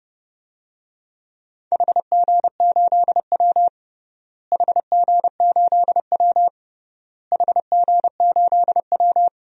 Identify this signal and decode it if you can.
Morse code, heard as a keyed tone, about 30 wpm, 700 Hz. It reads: HG8W HG8W HG8W
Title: DXCC-100 Call Signs - 4 Characters - Repeated 3x - 30wpm